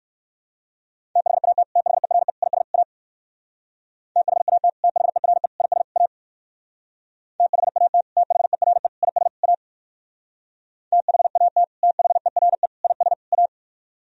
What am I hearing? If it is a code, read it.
Morse code, heard as a keyed tone, about 58 wpm, 700 Hz. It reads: THAT THERE IS A THAT THERE IS A THAT THERE IS A THAT THERE IS A